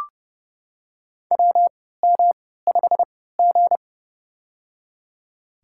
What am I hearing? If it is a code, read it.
Morse code, heard as a keyed tone, about 29 wpm, spaced as for 17 wpm, 700 Hz. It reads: WM5G
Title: USA Call Signs - 4 Characters - 3x Character Spacing - 30wpm